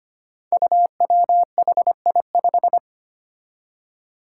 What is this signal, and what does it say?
Morse code, heard as a keyed tone, about 25 wpm, 700 Hz. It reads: UWHI5
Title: Five Challenging Letter-Number Combinations 25wpm